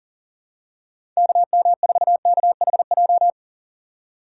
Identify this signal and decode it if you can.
Morse code, heard as a keyed tone, about 40 wpm, 700 Hz. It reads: KM4KHJ
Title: USA Call Signs - 4x Character Spacing - 40wpm